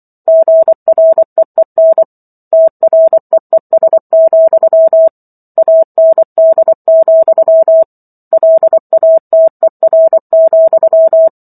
Morse code, heard as a keyed tone, about 24 wpm, 650 Hz. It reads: GREEN TREES, AND, LATER,